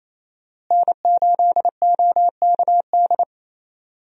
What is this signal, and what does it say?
Morse code, heard as a keyed tone, about 28 wpm, 700 Hz. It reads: N8OKD
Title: USA Call Signs - 5 Characters - Repeated 2x - 28wpm